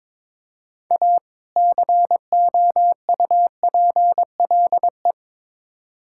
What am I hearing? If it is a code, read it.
Morse code, heard as a keyed tone, about 22 wpm, 700 Hz. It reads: A COUPLE
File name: practice / 22wpm / Build A Phrase - 5-Word Phrases - 22wpm